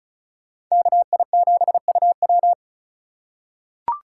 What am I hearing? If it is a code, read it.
Morse code, heard as a keyed tone, about 34 wpm, 700 Hz. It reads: KI7UW
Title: USA Call Signs - 5 Characters - 3x Character Spacing - 35wpm